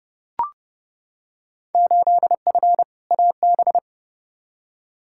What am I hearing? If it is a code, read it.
Morse code, heard as a keyed tone, about 30 wpm, 700 Hz. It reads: 8F AB
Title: Field Day Exchanges Encore - 30wpm